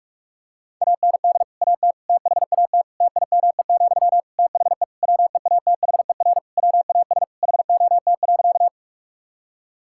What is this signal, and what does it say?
Morse code, heard as a keyed tone, about 41 wpm, 700 Hz. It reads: AND AT THAT TIME, THE WEATHER WAS HOT.
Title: Sentences from Top 1000 Words - Review Part 5 - 45wpm